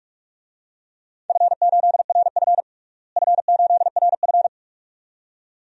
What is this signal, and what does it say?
Morse code, heard as a keyed tone, about 46 wpm, 700 Hz. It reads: F8RF F8RF
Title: DXCC-100 Call Signs - 4 Characters - Repeated 2x - 45wpm